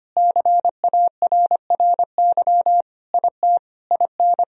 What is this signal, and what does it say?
Morse code, heard as a keyed tone, about 25 wpm, 700 Hz. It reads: CARRY IT IN
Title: Sentences from Top 1000 Words - Review Part 1 - 25wpm